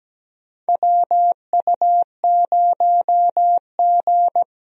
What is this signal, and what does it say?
Morse code, heard as a keyed tone, about 17 wpm, 700 Hz. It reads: WU0G
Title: USA Call Signs - 4 Characters - Repeated 2x - 17wpm